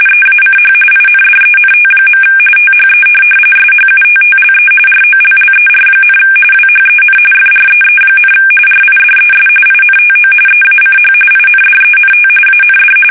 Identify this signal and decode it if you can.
RTTY, about 45.45 baud. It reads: EACH INDENT. THE SHUTTLE ROLLER, WHEN FREE, RIDES INTO EACH INDENT AND